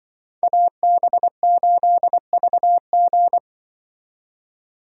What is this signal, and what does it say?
Morse code, heard as a keyed tone, about 24 wpm, 700 Hz. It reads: AB8VG
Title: Speed Racing - USA 2x2 Call Signs 20wpm